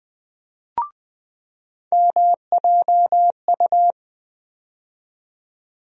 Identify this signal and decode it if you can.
Morse code, heard as a keyed tone, about 20 wpm, 700 Hz. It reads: MJU
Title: USA Call Sign Postfixes - 3 Letters 20wpm